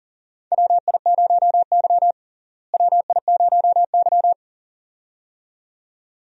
Morse code, heard as a keyed tone, about 40 wpm, 700 Hz. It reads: WI0Y WI0Y
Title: USA Call Signs - 4 Characters - Repeated 2x - 40wpm